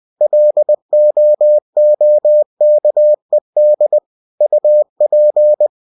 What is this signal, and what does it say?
Morse code, heard as a keyed tone, about 20 wpm, 600 Hz. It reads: LOOKED UP
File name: cw-books / winnie-the-pooh / winnie-20wpm-page-0062